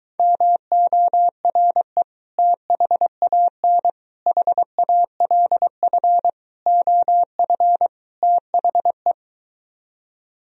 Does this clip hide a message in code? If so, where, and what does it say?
Yes: Morse code from 0.2 to 9.1 s, heard as a keyed tone, about 23 wpm, 700 Hz. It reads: MORE THAN HALF OF THE